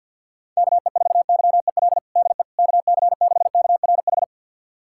Morse code, heard as a keyed tone, about 46 wpm, 700 Hz. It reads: KE4YEL DE KC6KRH
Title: USA Callsign DE Callsign 50wpm